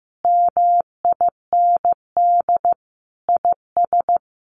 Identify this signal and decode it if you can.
Morse code, heard as a keyed tone, about 15 wpm, 700 Hz. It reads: MIND IS
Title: Sentences from Top 1500 Words 15wpm